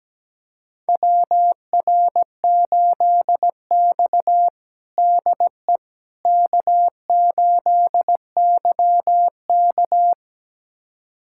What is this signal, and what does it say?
Morse code, heard as a keyed tone, about 17 wpm, 700 Hz. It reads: WR8X DE K8YK
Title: USA Callsign DE Callsign - 4 Characters - 17wpm